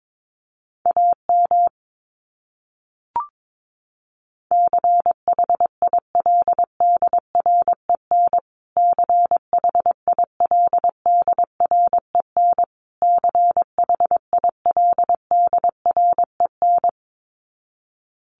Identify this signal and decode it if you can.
Morse code, heard as a keyed tone, about 22 wpm, 700 Hz. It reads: AM CHILDREN CHILDREN CHILDREN
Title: Top 500 Words Repeated 22wpm